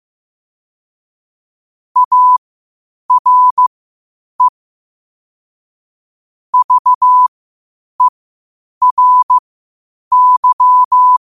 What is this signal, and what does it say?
Morse code, heard as a keyed tone, about 15 wpm, spaced as for 8 wpm, 1000 Hz. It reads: ARE VERY
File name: qotdfiles / 8wpm